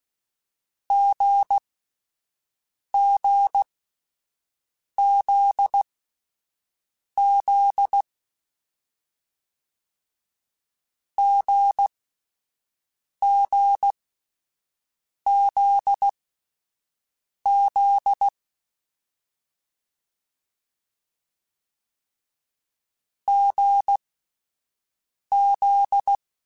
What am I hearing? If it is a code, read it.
Morse code, heard as a keyed tone, about 16 wpm, spaced as for 5 wpm, 800 Hz. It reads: GGZZ GGZZ GZ
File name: Difficult Combos / Difficult Combos-mp3s / GZ GZ7 @5